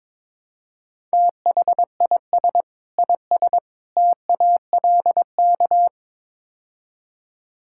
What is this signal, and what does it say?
Morse code, heard as a keyed tone, about 22 wpm, 700 Hz. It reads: THIS IS TALK
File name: practice / 22wpm / Build A Phrase - 5-Word Phrases - 22wpm